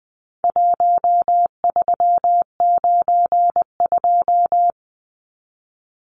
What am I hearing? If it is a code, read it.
Morse code, heard as a keyed tone, about 20 wpm, 700 Hz. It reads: 1392
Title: Numbers - Contest Style - 4-Digit 20wpm